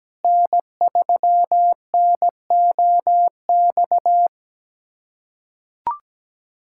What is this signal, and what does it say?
Morse code, heard as a keyed tone, about 17 wpm, 700 Hz. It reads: N3NOX
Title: USA Callsign DE Callsign - 5 Characters - 17wpm